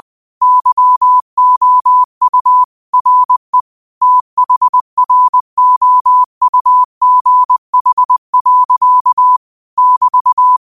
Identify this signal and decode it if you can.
Morse code, heard as a keyed tone, about 20 wpm, 1000 Hz. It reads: YOU RE THROUGH. <BT>
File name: 20wpm